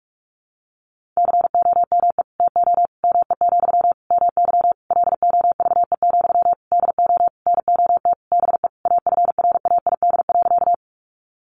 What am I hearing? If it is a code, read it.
Morse code, heard as a keyed tone, about 44 wpm, 700 Hz. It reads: COME TO ME, MY LOVE, DO NOT BE AFRAID.
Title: Sentences from Top 1500 Words 45wpm